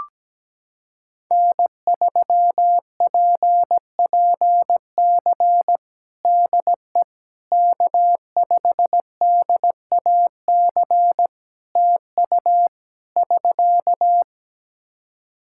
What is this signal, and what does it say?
Morse code, heard as a keyed tone, about 17 wpm, 700 Hz. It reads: N3PPC DE K5DAC TU <SK>
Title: K1USN SST QSO Practice 17wpm